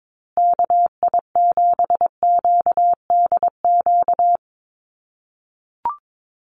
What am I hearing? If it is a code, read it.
Morse code, heard as a keyed tone, about 22 wpm, 700 Hz. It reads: KI7QDQ
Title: USA Call Signs Repeated 2x 22wpm